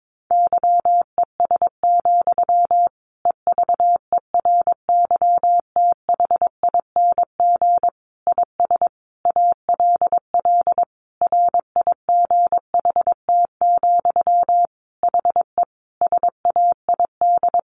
Morse code, heard as a keyed tone, about 22 wpm, 700 Hz. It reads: YES, EVERYTHING IS ALL RIGHT, HE SAID
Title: Sentences from Top 1100 Words 22wpm